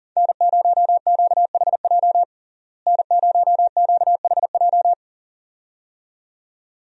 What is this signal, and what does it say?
Morse code, heard as a keyed tone, about 40 wpm, 700 Hz. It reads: N0QHJ N0QHJ
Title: USA Call Signs - 5 Characters - Repeated 2x - 40wpm